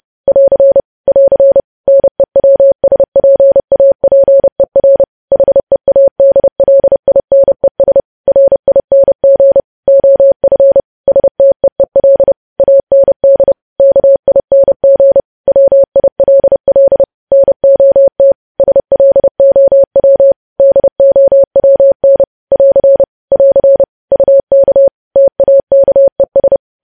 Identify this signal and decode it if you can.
Morse code, heard as a keyed tone, about 30 wpm, 550 Hz. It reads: <AR> <AR> NEWSPAPER HEADLINES RING OF STEEL AND KING WILL NOT SLOW DOWN <AR> <AR> UK TAKES
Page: news headlines in morse code at 30 wpm